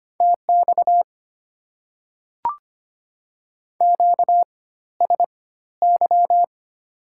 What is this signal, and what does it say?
Morse code, heard as a keyed tone, about 25 wpm, spaced as for 14 wpm, 690 Hz. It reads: TX QSY
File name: Top 90 QSO Elements - 4x Character Spacing - 25wpm